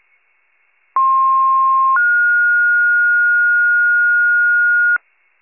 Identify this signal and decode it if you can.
Two-tone page, heard as a one-second tone, then a three-second tone, about 1060 then 1430 Hz, with noise. It is heard from 1.0 to 5.0 s.
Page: 23 FIRE